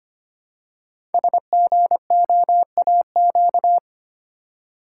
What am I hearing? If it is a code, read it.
Morse code, heard as a keyed tone, about 25 wpm, 700 Hz. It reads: SGOAQ
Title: Code Groups - 5 Letters - 25wpm